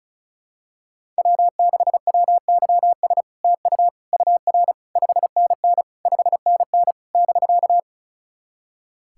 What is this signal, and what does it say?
Morse code, heard as a keyed tone, about 38 wpm, 700 Hz. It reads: W6WYS TU UR 5NN 5NN <BK>
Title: SOTA QSO Practice 35wpm